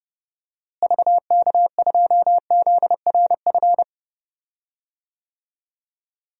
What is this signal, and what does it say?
Morse code, heard as a keyed tone, about 32 wpm, 700 Hz. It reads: VK2ZRF
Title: DXCC-100 Call Signs - 6 Characters - Repeated 2x - 30wpm